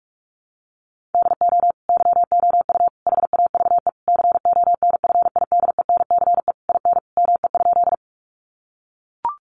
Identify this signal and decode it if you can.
Morse code, heard as a keyed tone, about 44 wpm, 700 Hz. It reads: DO YOU HAVE CONFIDENCE IN ME?
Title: Sentences from Top 1500 Words 45wpm